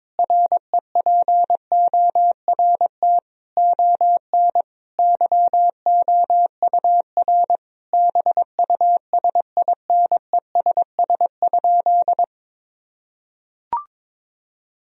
Morse code, heard as a keyed tone, about 22 wpm, 700 Hz. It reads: REPORT ON YOUR BUSINESS?
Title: Sentences from Top 1000 Words - Review Part 4 - 22wpm